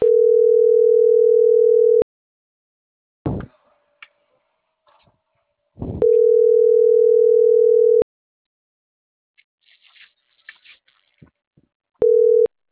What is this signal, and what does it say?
Ringback tone (North America), 440 + 480 Hz.